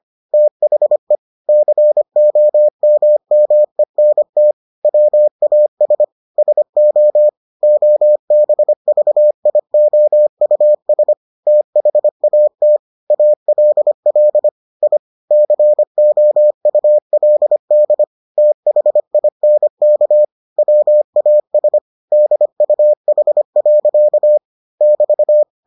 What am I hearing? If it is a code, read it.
Morse code, heard as a keyed tone, about 25 wpm, 600 Hz. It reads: THE COMMENT WAS SO OBVIOUS THAT ALL I COULD THINK WAS DUH. <BT>